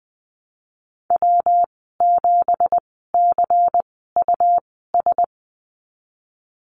Morse code, heard as a keyed tone, about 20 wpm, spaced as for 15 wpm, 700 Hz. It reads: W7CUS
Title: 193 Lesson 31g - Letter v - Call Signs 20wpm